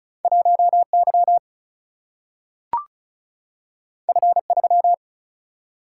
Morse code, heard as a keyed tone, about 36 wpm, 700 Hz. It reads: 1 Y F 3
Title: Single-Letter Single-Number 35wpm